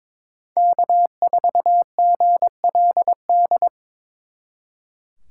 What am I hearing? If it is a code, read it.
Morse code, heard as a keyed tone, about 25 wpm, 700 Hz. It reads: K4GLD
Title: Speed Racing - USA 1x3 Call Signs 22wpm